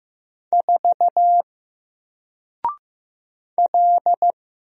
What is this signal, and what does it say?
Morse code, heard as a keyed tone, about 15 wpm, 700 Hz. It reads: 4 L
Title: Single Letter-Number 15wpm